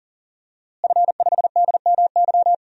Morse code, heard as a keyed tone, about 40 wpm, 700 Hz. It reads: F5DMY
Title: DXCC-100 Call Signs - 5 Characters - Repeated 2x - 40wpm